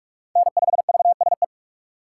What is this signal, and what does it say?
Morse code, heard as a keyed tone, about 46 wpm, 700 Hz. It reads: N5VSE
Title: USA Call Signs - 5 Characters - 4x Character Spacing - 45wpm